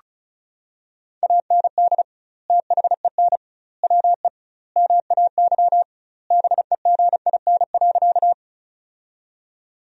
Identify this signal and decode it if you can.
Morse code, heard as a keyed tone, about 36 wpm, 700 Hz. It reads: AND THEN WE MAY BEGIN.
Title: 2x Word-Spacing - Sentences from Top 300 Words - 35wpm